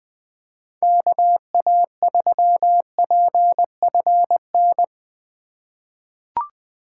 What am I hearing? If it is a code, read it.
Morse code, heard as a keyed tone, about 21 wpm, 700 Hz. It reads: KA3PFN
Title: USA Call Signs - 2x Character Spacing - 20wpm